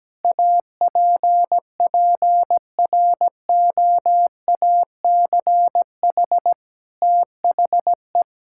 Morse code, heard as a keyed tone, about 17 wpm, 700 Hz. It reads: APPROACH THE
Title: Sentences from Top 1000 Words - Review Part 6 - 17wpm